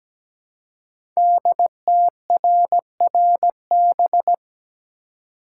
Morse code, heard as a keyed tone, about 17 wpm, 700 Hz. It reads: DTRRB
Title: Code Groups - 5 Letters - 17wpm